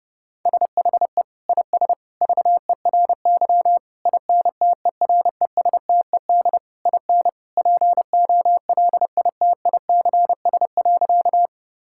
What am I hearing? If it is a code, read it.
Morse code, heard as a keyed tone, about 30 wpm, 700 Hz. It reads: SHE IS VERY INTERESTED IN POLITICS.
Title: Sentences from Top 1000 Words - Review Part 6 - 30wpm